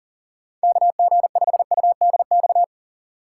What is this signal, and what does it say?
Morse code, heard as a keyed tone, about 37 wpm, 700 Hz. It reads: KG5UDX
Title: USA Call Signs - 6 Characters - 2x Character Spacing - 40wpm